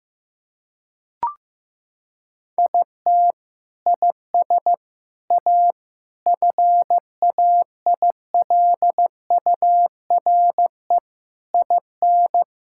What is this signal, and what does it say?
Morse code, heard as a keyed tone, about 15 wpm, 700 Hz. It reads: IT IS A FAILURE IN
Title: Sentences from Top 1000 Words - Review Part 6 - 15wpm